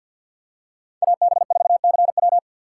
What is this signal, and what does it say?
Morse code, heard as a keyed tone, about 51 wpm, 700 Hz. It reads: AB4CW
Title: USA Call Signs - 5 Characters - 3x Character Spacing - 50wpm